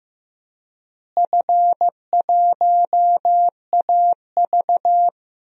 Morse code, heard as a keyed tone, about 16 wpm, 700 Hz. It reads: F1AV
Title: DXCC-100 Call Signs - 4 Characters - Repeated 3x - 15wpm